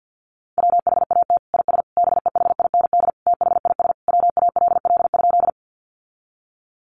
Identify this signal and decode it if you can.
Morse code, heard as a keyed tone, about 50 wpm, 700 Hz. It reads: WHAT IS BEHIND THIS WALL?
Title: Sentences from Top 1100 Words 50wpm